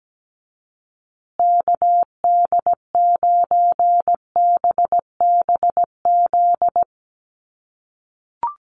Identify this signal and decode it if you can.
Morse code, heard as a keyed tone, about 17 wpm, 700 Hz. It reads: KD9BBZ
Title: USA Call Signs Repeated 3x 17wpm